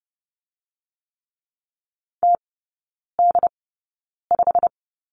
Morse code, heard as a keyed tone, about 30 wpm, spaced as for 9 wpm, 700 Hz. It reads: TD5
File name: course / 30:10wpm / 087 Lesson 15e - Number 5 - Three Characters 30wpm